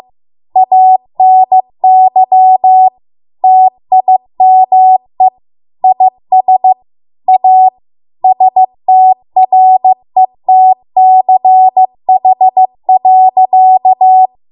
Morse code, heard as a keyed tone, about 15 wpm, 750 Hz. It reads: ANY TIME IS A STRETCH.